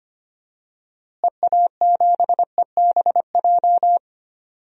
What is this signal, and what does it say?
Morse code, heard as a keyed tone, about 25 wpm, 700 Hz. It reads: EA7EBJ
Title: DXCC-100 Call Signs - 6 Characters - Repeated 2x - 25wpm